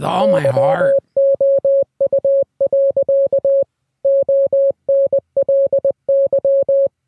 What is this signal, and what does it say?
Morse code, heard as a keyed tone, about 20 wpm, 550 Hz, with speech. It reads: YOU. ONLY